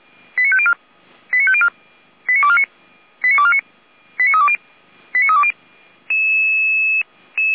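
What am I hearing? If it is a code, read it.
Five-tone selective call (ZVEI), heard as quick runs of five tones, tones of about 70 ms. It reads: 89504 89504 89249 89249 89320 89320
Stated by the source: ZVEI five-tone call received on an SDR.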